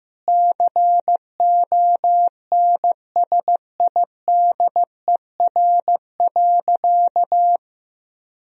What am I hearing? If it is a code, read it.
Morse code, heard as a keyed tone, about 15 wpm, 700 Hz. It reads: CONSIDER.
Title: Sentences from Top 1000 Words - Review Part 4 - 15wpm